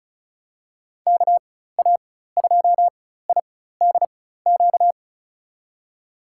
Morse code, heard as a keyed tone, about 36 wpm, spaced as for 16 wpm, 700 Hz. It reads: KA2IDQ
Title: USA Call Signs - 6 Characters - 4x Character Spacing - 35wpm